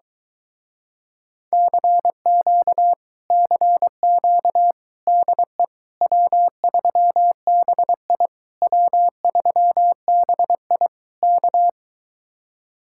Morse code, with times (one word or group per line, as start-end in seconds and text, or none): 1.5-2.9 CQ
3.3-4.7 CQ
5.1-5.6 DE
6.0-8.3 W3BI
8.6-10.9 W3BI
11.2-11.7 K